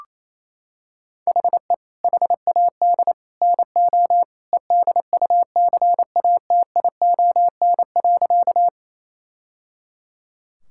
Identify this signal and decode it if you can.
Morse code, heard as a keyed tone, about 28 wpm, 700 Hz. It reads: HE HAD NO EDUCATION.
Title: Sentences from Top 1000 Words - Review Part 6 - 28wpm